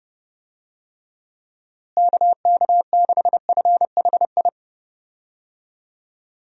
Morse code, heard as a keyed tone, about 30 wpm, 700 Hz. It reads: KK6FHI
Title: USA Call Signs - 6 Characters - Repeated 2x - 30wpm